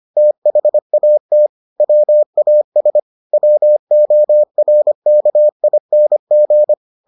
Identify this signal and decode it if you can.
Morse code, heard as a keyed tone, about 25 wpm, 600 Hz. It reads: THAT WAS WORKING